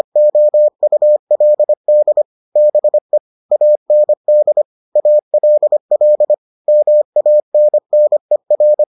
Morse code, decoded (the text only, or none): OULD BE AND ALL MANNER